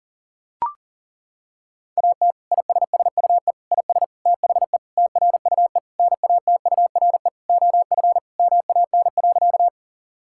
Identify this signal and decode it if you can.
Morse code, heard as a keyed tone, about 40 wpm, 700 Hz. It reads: AT ISSUE IS THE TRUE NATURE OF MAN.
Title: Sentences from Top 1000 Words - Review Part 6 - 40wpm